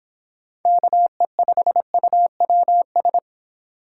Morse code, heard as a keyed tone, about 26 wpm, 700 Hz. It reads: KE5UWS